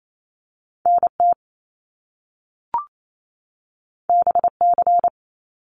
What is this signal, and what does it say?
Morse code, heard as a keyed tone, about 28 wpm, 700 Hz. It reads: NT BC
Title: Canadian Provinces and Territories 28wpm